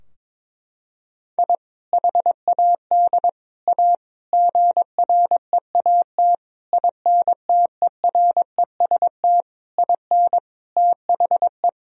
Morse code, heard as a keyed tone, about 22 wpm, 700 Hz. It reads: I HAD A GREAT INTEREST IN THE
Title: Sentences from Top 1000 Words - Review Part 2 - 22wpm